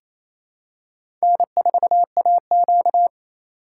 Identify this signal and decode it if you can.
Morse code, heard as a keyed tone, about 28 wpm, 700 Hz. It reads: N4AQ
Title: USA Call Signs - 2x Character Spacing - 28wpm